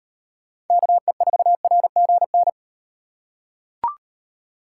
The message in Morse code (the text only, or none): KE4RGN